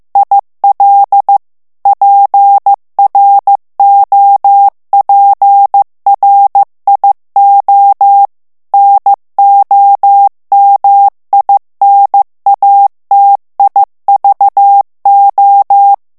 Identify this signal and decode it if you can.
Morse code, heard as a keyed tone, about 15 wpm, 800 Hz. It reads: IL PROPRIO NOMINATIVO
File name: morse mp3 / ITALIEN texte clair 2 vitesse 14 mots minute